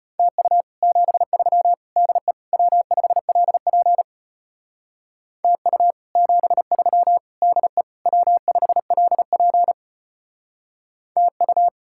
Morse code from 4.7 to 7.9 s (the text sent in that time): TU 73 DE